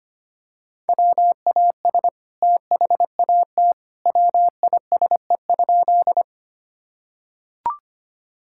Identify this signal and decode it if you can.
Morse code, heard as a keyed tone, about 25 wpm, 700 Hz. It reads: WAS THAT WISE?
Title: Sentences from Top 1600 Words 25wpm